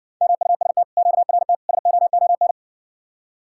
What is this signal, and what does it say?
Morse code, heard as a keyed tone, about 60 wpm, 700 Hz. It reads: MUST QRT SOON